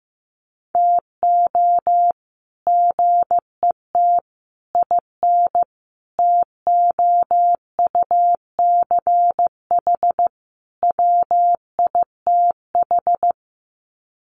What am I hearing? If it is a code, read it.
Morse code, heard as a keyed tone, about 15 wpm, 700 Hz. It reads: TO GET IN TOUCH WITH
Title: Sets of 5 Words Encore2 15wpm